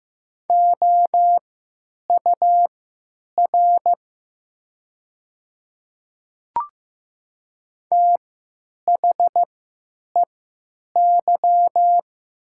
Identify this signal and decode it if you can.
Morse code, heard as a keyed tone, about 15 wpm, spaced as for 9 wpm, 700 Hz. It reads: OUR THEY
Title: 3x Character Spacing - Top 200 Words - 15wpm